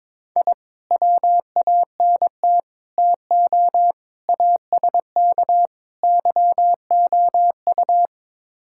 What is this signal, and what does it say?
Morse code, heard as a keyed tone, about 22 wpm, 700 Hz. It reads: I WANT TO ASK YOU
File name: practice / 22wpm / Build A Phrase - 5-Word Phrases - 22wpm